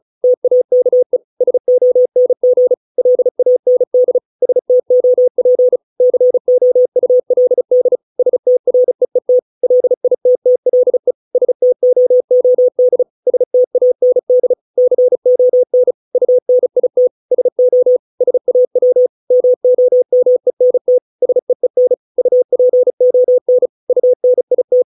Morse code, heard as a keyed tone, about 35 wpm, 500 Hz. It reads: TAKE SONG LAND STOP COULD STREET LITTLE STOOD STAND CON UNIT SO SAW MOMENT SEEN UPON UNIT